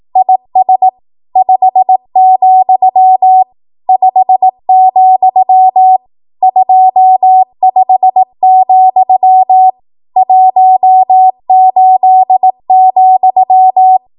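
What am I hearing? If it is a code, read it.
Morse code, heard as a keyed tone, about 18 wpm, 750 Hz. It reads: IS 5, 5, 25, 18,
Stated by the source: ARRL 18 wpm Morse code practice archive transmission.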